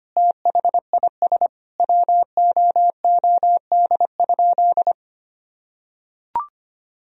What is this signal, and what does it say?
Morse code, heard as a keyed tone, about 25 wpm, 700 Hz. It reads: THIS WOOD?
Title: Sentences from Top 1000 Words - Review Part 2 - 25wpm